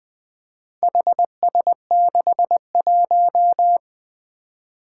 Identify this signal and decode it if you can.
Morse code, heard as a keyed tone, about 20 wpm, 700 Hz. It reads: HS61
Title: Four Challenging Letter-Number Combinations 20wpm